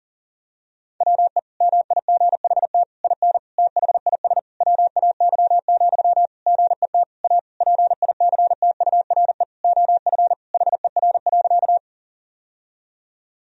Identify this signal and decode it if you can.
Morse code, heard as a keyed tone, about 37 wpm, 700 Hz. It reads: WE MIGHT IN THIS WAY, GET A PICTURE OF HER.
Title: Sentences from Top 1000 Words - Review Part 1 - 40wpm